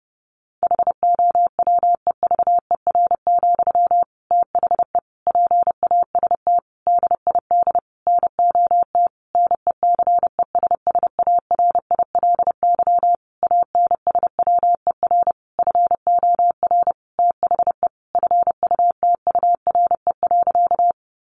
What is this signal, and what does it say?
Morse code, heard as a keyed tone, about 30 wpm, 700 Hz. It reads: HOWEVER, THE PAST DID NOT NECESSARILY ANSWER FOR THE FUTURE.
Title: Sentences from Top 1400 Words 30wpm